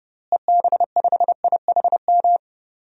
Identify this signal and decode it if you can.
Morse code, heard as a keyed tone, about 30 wpm, 700 Hz. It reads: EB5IHM
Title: DXCC-100 Call Signs - 6 Characters - Repeated 3x - 30wpm